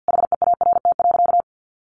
Morse code, heard as a keyed tone, about 48 wpm, 700 Hz. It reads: HEART.
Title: Sentences from Top 1500 Words 50wpm